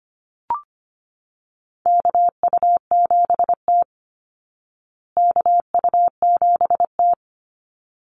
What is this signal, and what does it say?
Morse code, heard as a keyed tone, about 25 wpm, 700 Hz. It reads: KU7T KU7T
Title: Call Signs Repeated 3x - 25wpm